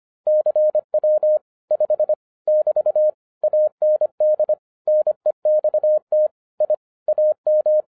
Morse code, heard as a keyed tone, about 25 wpm, 610 Hz. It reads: CW 5 <BT> AND NEXT I AM
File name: 2011-07-mp3 / n24 QRP OTAKU 25wpm